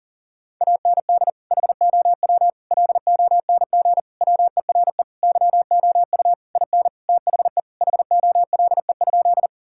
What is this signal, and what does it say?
Morse code, heard as a keyed tone, about 40 wpm, 700 Hz. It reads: AND HOW LONG WERE YOU IN THE HOLE?
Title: Sentences from Top 1600 Words 40wpm